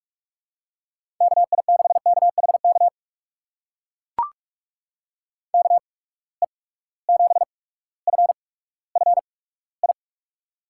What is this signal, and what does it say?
Morse code, heard as a keyed tone, about 44 wpm, spaced as for 19 wpm, 700 Hz. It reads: KI6KHK KE7FFI
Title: USA Call Signs - 6 Characters - 8x Character Spacing - 45wpm